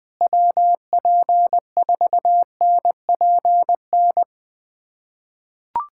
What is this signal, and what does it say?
Morse code, heard as a keyed tone, about 20 wpm, 700 Hz. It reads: WP4NPN
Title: USA Callsign DE Callsign 20wpm